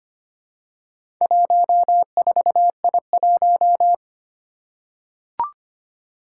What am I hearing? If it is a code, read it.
Morse code, heard as a keyed tone, about 25 wpm, 700 Hz. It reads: 14I1
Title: Four Challenging Letter-Number Combinations 25wpm